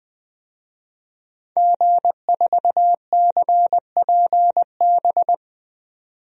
Morse code, heard as a keyed tone, about 20 wpm, 700 Hz. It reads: G4CPB
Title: DXCC-100 Call Signs - 5 Characters - 20wpm